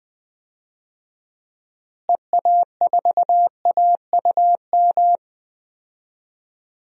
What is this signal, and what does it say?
Morse code, heard as a keyed tone, about 20 wpm, 700 Hz. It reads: EA4AUM
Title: DXCC-100 Call Signs - 6 Characters - Repeated 3x - 20wpm